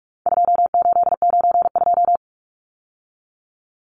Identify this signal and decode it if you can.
Morse code, heard as a keyed tone, about 44 wpm, 700 Hz. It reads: 2892
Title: Numbers - 4-Digit 45wpm